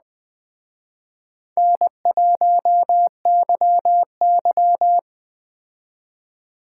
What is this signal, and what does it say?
Morse code, heard as a keyed tone, about 20 wpm, 700 Hz. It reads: N1YY
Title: Speed Racing - USA 1x2 Call Signs 17wpm